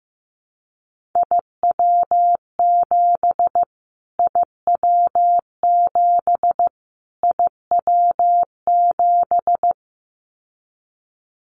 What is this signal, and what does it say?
Morse code, heard as a keyed tone, about 15 wpm, 700 Hz. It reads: IW7 IW7 IW7
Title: Callsign Prefixes for Top 25 DXCC - Repeated 3x - 15wpm